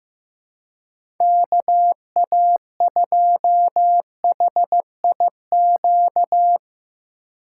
Morse code, heard as a keyed tone, about 15 wpm, 700 Hz. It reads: KA2HIQ
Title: USA Call Signs - 6 Characters - 4x Character Spacing - 15wpm